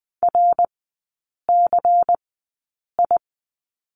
Morse code, heard as a keyed tone, about 20 wpm, spaced as for 8 wpm, 700 Hz. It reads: RCI
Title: 114 Lesson 19e - Letter w - Three Characters 20wpm